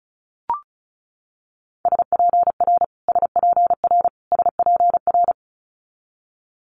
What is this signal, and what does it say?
Morse code, heard as a keyed tone, about 36 wpm, 700 Hz. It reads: SPR SPR SPR
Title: Multi-letter Phonemes Repeated 3x 35wpm